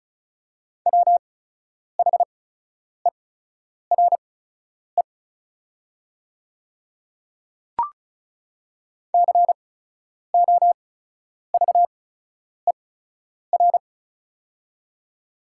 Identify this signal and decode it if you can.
Morse code, heard as a keyed tone, about 36 wpm, spaced as for 10 wpm, 690 Hz. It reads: WHERE COVER
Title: 8x Character Spacing - Top 500 Words - 35wpm